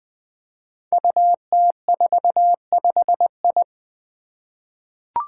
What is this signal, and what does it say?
Morse code, heard as a keyed tone, about 20 wpm, 700 Hz. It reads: UT45I
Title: Five Challenging Letter-Number Combinations 20wpm